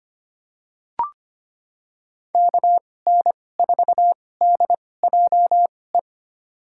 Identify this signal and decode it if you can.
Morse code, heard as a keyed tone, about 25 wpm, spaced as for 18 wpm, 700 Hz. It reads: KN4DJE
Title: USA Call Signs - 6 Characters - 2x Character Spacing - 25wpm